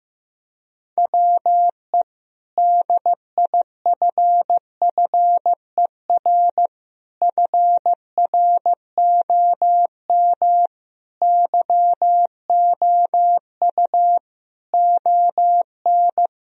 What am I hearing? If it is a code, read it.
Morse code, heard as a keyed tone, about 15 wpm, 700 Hz. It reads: WE DIFFER FROM YOU ON
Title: Sentences from Top 1000 Words - Review Part 1 - 15wpm